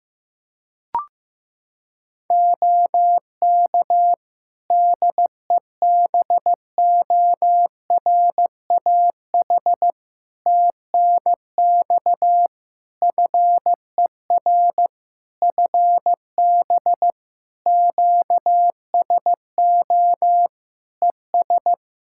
Morse code, heard as a keyed tone, about 15 wpm, 700 Hz. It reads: OK DEBORAH TNX FER FB QSO ES
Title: Practice QSO - Exchange 4 - Ending - 15wpm